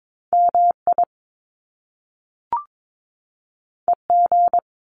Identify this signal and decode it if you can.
Morse code, heard as a keyed tone, about 22 wpm, 700 Hz. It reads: MI EG
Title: Top Two Letter Combinations in Words 22wpm